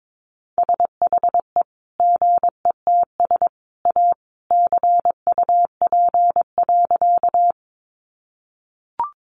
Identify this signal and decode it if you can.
Morse code, heard as a keyed tone, about 22 wpm, 700 Hz. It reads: SHE GETS A CUP.
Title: Additional Easy Phrases 22wpm